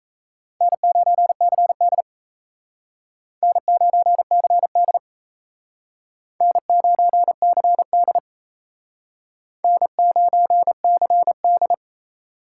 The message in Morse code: N9CD N9CD N9CD N9CD

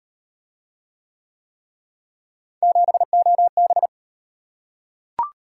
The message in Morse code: 7OB

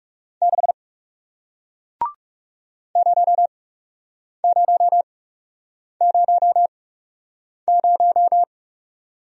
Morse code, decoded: 6 0000

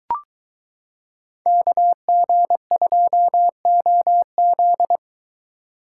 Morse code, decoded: KG2OZ

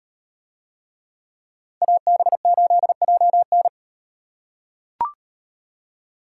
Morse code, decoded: AB8JN

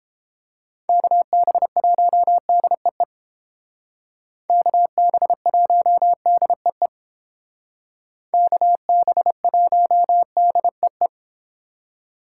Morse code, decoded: KB1DEE KB1DEE KB1DEE

